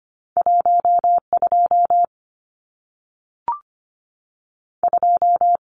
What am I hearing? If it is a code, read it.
Morse code, heard as a keyed tone, about 25 wpm, 700 Hz. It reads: 12 2
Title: Numbers - 2-Digit 25wpm